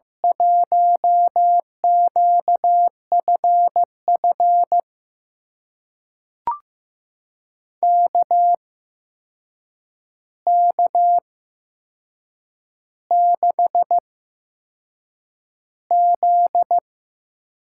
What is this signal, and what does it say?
Morse code, heard as a keyed tone, about 15 wpm, spaced as for 6 wpm, 700 Hz. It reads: 1QFF KK6Z
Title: USA Call Signs - 6 Characters - 8x Character Spacing - 15wpm